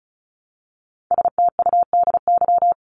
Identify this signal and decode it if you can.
Morse code, heard as a keyed tone, about 34 wpm, 700 Hz. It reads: STUDY
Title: ES:S Suffix - Top 100 Words with Base Words 35wpm